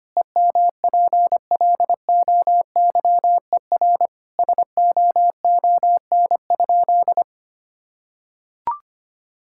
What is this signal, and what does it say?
Morse code, heard as a keyed tone, about 25 wpm, 700 Hz. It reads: EMPLOYER SOON?
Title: Sentences from Top 1600 Words 25wpm